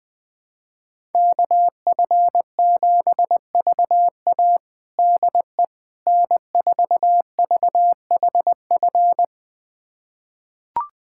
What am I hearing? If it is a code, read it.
Morse code, heard as a keyed tone, about 20 wpm, 700 Hz. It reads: KF7VA DE N4VHF
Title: USA Callsign DE Callsign - 5 Characters - 20wpm